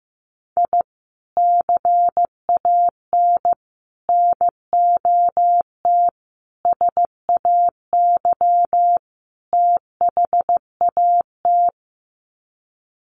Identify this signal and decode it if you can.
Morse code, heard as a keyed tone, about 15 wpm, 700 Hz. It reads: I CAN NOT SAY THAT
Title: Sets of 5 Words Encore2 15wpm